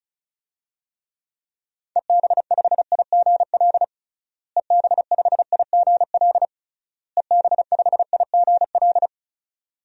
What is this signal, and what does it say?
Morse code, heard as a keyed tone, about 36 wpm, 700 Hz. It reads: EB5IGL EB5IGL EB5IGL
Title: DXCC-100 Call Signs - 6 Characters - Repeated 3x - 35wpm